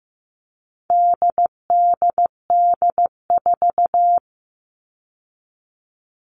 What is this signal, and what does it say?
Morse code, heard as a keyed tone, about 15 wpm, 700 Hz. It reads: DDD4
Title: All Cut Numbers - 4-Digit 15wpm